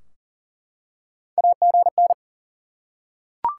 Morse code, heard as a keyed tone, about 40 wpm, 700 Hz. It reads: AGN
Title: Top 90 QSO Elements 40wpm